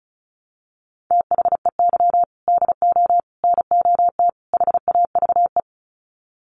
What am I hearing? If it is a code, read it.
Morse code, heard as a keyed tone, about 34 wpm, 700 Hz. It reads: THEY DO NOT HAVE
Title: Sets of 4 Words 35wpm